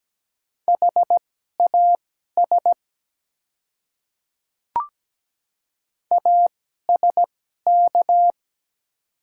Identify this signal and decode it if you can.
Morse code, heard as a keyed tone, about 17 wpm, spaced as for 12 wpm, 700 Hz. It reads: HAS ASK
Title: 2x Character Spacing - Top 200 Words - 17wpm